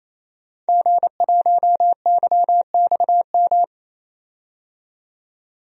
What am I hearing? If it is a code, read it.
Morse code, heard as a keyed tone, about 28 wpm, 700 Hz. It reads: G1YXM
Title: DXCC-100 Call Signs - 5 Characters - 28wpm